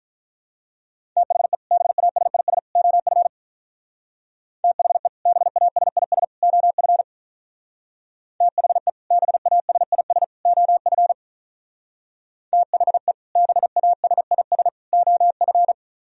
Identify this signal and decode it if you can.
Morse code, heard as a keyed tone, about 44 wpm, 700 Hz. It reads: THE BASIS OF THE BASIS OF THE BASIS OF THE BASIS OF